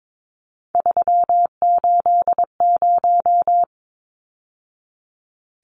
Morse code, heard as a keyed tone, about 22 wpm, 700 Hz. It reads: 380
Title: Numbers - 3-Digit 22wpm